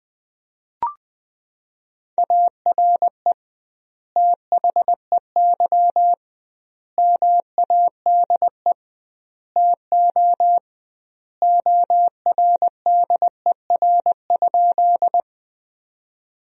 Morse code, heard as a keyed tone, about 21 wpm, 700 Hz. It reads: ARE THEY MADE TO ORDER?
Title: 2x Word-Spacing - Sentences from Top 400 Words - 20wpm